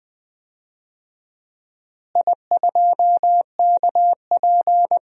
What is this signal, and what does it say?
Morse code, heard as a keyed tone, about 20 wpm, 700 Hz. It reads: I2KP
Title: DXCC-100 Call Signs - 4 Characters - Repeated 3x - 20wpm